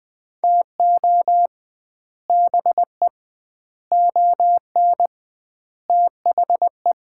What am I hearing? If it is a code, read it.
Morse code, heard as a keyed tone, about 21 wpm, 700 Hz. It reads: TO BE ON THE
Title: 2x Word-Spacing - Sets of 4 Words - 20wpm